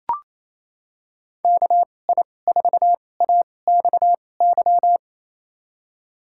Morse code, heard as a keyed tone, about 28 wpm, spaced as for 20 wpm, 700 Hz. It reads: KI4AXY
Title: USA Call Signs - 2x Character Spacing - 28wpm